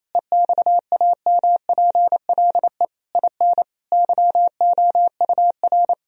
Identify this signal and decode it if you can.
Morse code, heard as a keyed tone, about 28 wpm, 700 Hz. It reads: EXAMPLE IN YOUR
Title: Sentences from Top 1000 Words - Review Part 1 - 28wpm